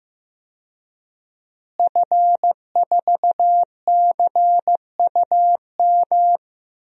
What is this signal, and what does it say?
Morse code, heard as a keyed tone, about 15 wpm, 700 Hz. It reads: F4CUM
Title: DXCC-100 Call Signs - 5 Characters - Repeated 2x - 15wpm